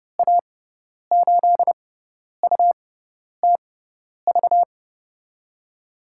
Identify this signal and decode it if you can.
Morse code, heard as a keyed tone, about 31 wpm, spaced as for 10 wpm, 700 Hz. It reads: A8UTV
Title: USA Call Signs - 6 Characters - 6x Character Spacing - 30wpm